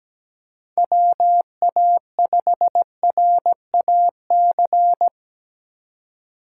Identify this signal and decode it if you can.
Morse code, heard as a keyed tone, about 17 wpm, 700 Hz. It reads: WA5RAC
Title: USA Call Signs - 2x Character Spacing - 17wpm